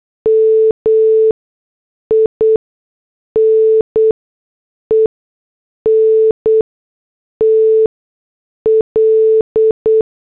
Morse code, heard as a keyed tone, about 8 wpm, spaced as for 6 wpm, 440 Hz. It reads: MINENT L
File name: morse / headlines